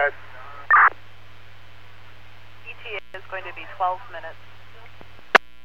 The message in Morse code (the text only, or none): none